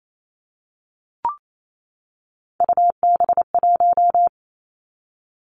Morse code, heard as a keyed tone, about 28 wpm, 700 Hz. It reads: UB1
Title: Three Challenging Letter-Number Combinations 28wpm